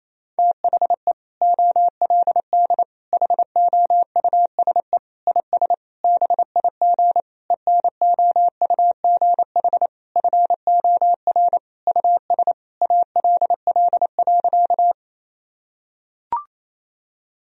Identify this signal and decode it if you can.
Morse code, heard as a keyed tone, about 28 wpm, 700 Hz. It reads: THE OLD HOUSE IS BIG ENOUGH FOR US ALL.
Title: Sentences from Top 1000 Words - Review Part 1 - 28wpm